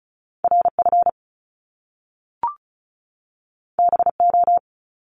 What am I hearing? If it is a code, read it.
Morse code, heard as a keyed tone, about 36 wpm, 700 Hz. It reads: RF BO
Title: Top Two Letter Combinations in Words 35wpm